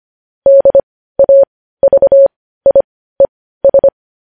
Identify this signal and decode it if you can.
Morse code, heard as a keyed tone, about 25 wpm, spaced as for 15 wpm, 550 Hz. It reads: DAVIES
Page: news headlines in morse code at 15 wpm